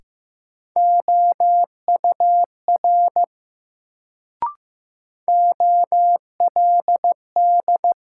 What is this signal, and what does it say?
Morse code, heard as a keyed tone, about 15 wpm, 700 Hz. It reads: OUR OLD
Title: Sentences from Top 200 Words - New Word Drill - 15wpm